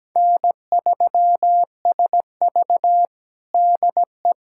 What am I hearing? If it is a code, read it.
Morse code, heard as a keyed tone, about 17 wpm, 700 Hz. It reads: N3SV DE
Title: USA Callsign DE Callsign - 4 Characters - 17wpm